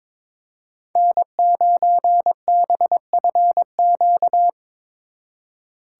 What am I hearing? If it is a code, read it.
Morse code, heard as a keyed tone, about 26 wpm, 700 Hz. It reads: N9BFQ
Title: Speed Racing - USA 1x3 Call Signs 22wpm